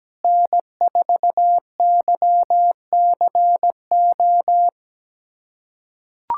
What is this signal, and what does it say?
Morse code, heard as a keyed tone, about 17 wpm, 700 Hz. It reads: N4YCO
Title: USA Callsign DE Callsign - 5 Characters - 17wpm